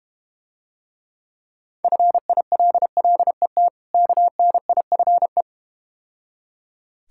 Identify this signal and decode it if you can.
Morse code, heard as a keyed tone, about 32 wpm, 700 Hz. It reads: FILLET KNIFE